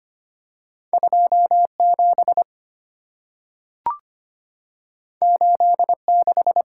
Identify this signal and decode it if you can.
Morse code, heard as a keyed tone, about 25 wpm, 700 Hz. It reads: 27 86
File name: Two Letter-Number Combinations - 25wpm